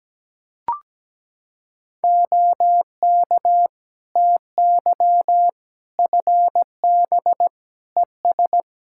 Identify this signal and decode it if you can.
Morse code, heard as a keyed tone, about 17 wpm, 700 Hz. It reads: OK TY FB ES
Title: K1USN SST QSO Practice 17wpm